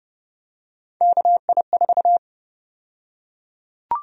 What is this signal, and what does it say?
Morse code, heard as a keyed tone, about 30 wpm, 700 Hz. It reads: KI4
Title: USA Call Sign Prefixes - Repeated 2x - 30wpm